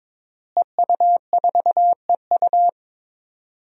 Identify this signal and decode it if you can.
Morse code, heard as a keyed tone, about 22 wpm, 700 Hz. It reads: EU4EU
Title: DXCC-100 Call Signs - 5 Characters - Repeated 2x - 22wpm